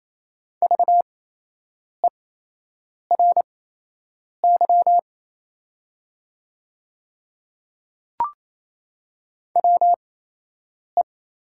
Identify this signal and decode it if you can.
Morse code, heard as a keyed tone, about 28 wpm, spaced as for 8 wpm, 700 Hz. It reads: VERY WE
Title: 8x Character Spacing - Top 300 Words - 28wpm